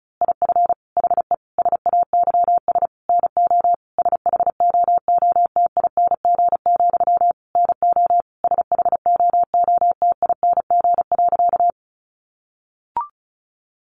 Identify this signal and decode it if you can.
Morse code, heard as a keyed tone, about 34 wpm, 700 Hz. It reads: IF HE SAYS NO SHOOTING, NO SHOOTING.
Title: A Sound Of Thunder 35wpm